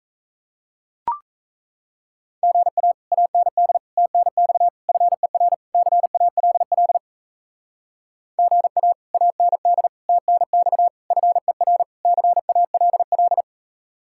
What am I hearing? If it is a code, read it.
Morse code, heard as a keyed tone, about 41 wpm, 700 Hz. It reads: GA AND TNX FER CALL GA AND TNX FER CALL